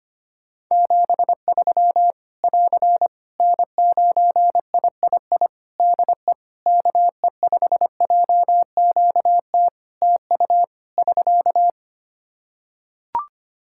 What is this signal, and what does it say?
Morse code, heard as a keyed tone, about 25 wpm, 700 Hz. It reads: 73 <AR> N9III DE KE5JQT TU <SK>
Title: Practice QSO - Exchange 4 - Ending - 25wpm